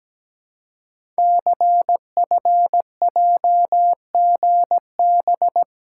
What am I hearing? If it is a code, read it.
Morse code, heard as a keyed tone, about 17 wpm, 700 Hz. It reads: CFJGB
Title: Code Groups - 5 Letters - 17wpm